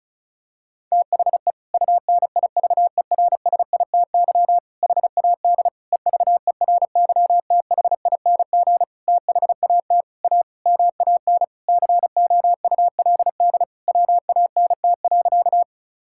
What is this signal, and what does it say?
Morse code, heard as a keyed tone, about 36 wpm, 700 Hz. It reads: THE UNIVERSITY HAD EVERYTHING THAT A MAN COULD WANT.
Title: Sentences from Top 1000 Words - Review Part 5 - 35wpm